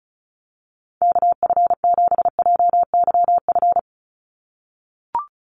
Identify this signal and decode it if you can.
Morse code, heard as a keyed tone, about 34 wpm, 700 Hz. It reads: KF7JYF
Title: USA Call Signs 35wpm